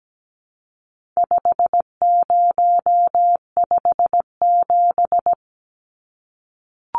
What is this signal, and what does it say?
Morse code, heard as a keyed tone, about 17 wpm, 700 Hz. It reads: 5057
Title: Numbers - 4-Digit 17wpm